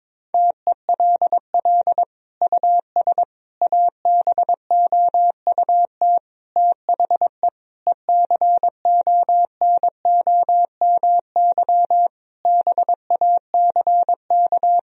Morse code, heard as a keyed tone, about 22 wpm, 700 Hz. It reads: TELL US ABOUT THE ECONOMY BACK
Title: Sentences from Top 1000 Words - Review Part 3 - 22wpm